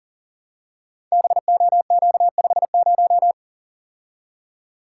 Morse code, heard as a keyed tone, about 43 wpm, 700 Hz. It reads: BOQ50
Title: Code Groups - 5 Letter:Numbers - 40wpm